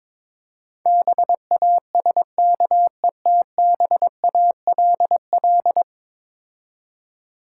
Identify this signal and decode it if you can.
Morse code, heard as a keyed tone, about 22 wpm, 700 Hz. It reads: BASKETBALL
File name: practice / 22wpm / Difficult Words with Hints - 22wpm